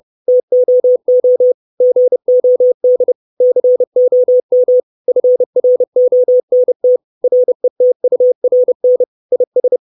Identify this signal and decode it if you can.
Morse code, heard as a keyed tone, about 30 wpm, 500 Hz. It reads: TOO GOD COM FRONT RETURN IS
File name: 632 Words 30WPM